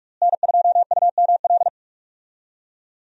Morse code, heard as a keyed tone, about 43 wpm, 700 Hz. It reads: N2UML
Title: USA Call Signs - 5 Characters - Repeated 3x - 45wpm